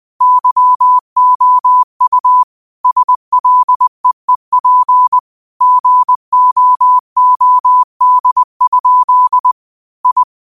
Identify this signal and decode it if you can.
Morse code, heard as a keyed tone, about 20 wpm, 1000 Hz. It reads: YOU SLEEP GOOD? I
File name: qotdfiles / 20wpm